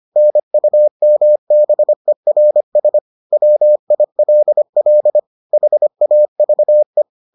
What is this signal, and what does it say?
Morse code, heard as a keyed tone, about 25 wpm, 600 Hz. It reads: NUMBERS WILL HAVE